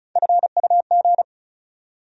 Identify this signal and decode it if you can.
Morse code, heard as a keyed tone, about 36 wpm, 700 Hz. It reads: FUG
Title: USA Call Sign Postfixes - 3 Letters 35wpm